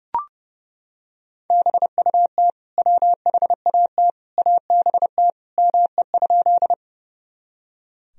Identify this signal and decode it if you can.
Morse code, heard as a keyed tone, about 30 wpm, 700 Hz. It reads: BUT WHAT ABT ME?
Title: Sentences from Top 200 Words and Abbreviations 30wpm